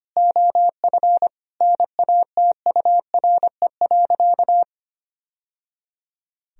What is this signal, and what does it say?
Morse code, heard as a keyed tone, about 25 wpm, 700 Hz. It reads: OF NATURE.
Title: Sentences from Top 1000 Words - Review Part 3 - 25wpm